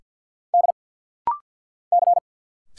Morse code, heard as a keyed tone, about 50 wpm, 700 Hz. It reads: D C
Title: Single Letter-Number - Rapid-Fire 50wpm